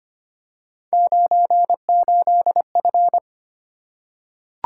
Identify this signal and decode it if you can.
Morse code, heard as a keyed tone, about 25 wpm, 700 Hz. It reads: 98F
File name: Three Letter-Number Combinations - 25wpm